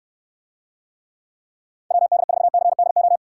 Morse code, heard as a keyed tone, about 68 wpm, 700 Hz. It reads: WD4CDO